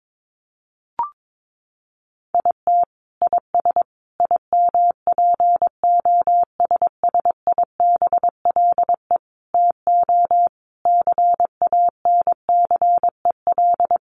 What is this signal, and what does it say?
Morse code, heard as a keyed tone, about 22 wpm, 700 Hz. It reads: IT IS IMPOSSIBLE TO CANCEL
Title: Sentences from Top 1200 Words 22wpm